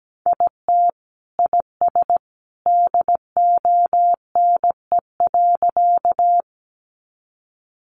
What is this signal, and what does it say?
Morse code, heard as a keyed tone, about 17 wpm, 700 Hz. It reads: IT IS DONE.
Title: Sentences from Top 500 Words 17wpm